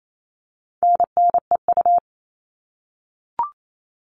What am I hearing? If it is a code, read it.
Morse code, heard as a keyed tone, about 28 wpm, 700 Hz. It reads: NNEU
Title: All Cut Numbers - 4-Digit 28wpm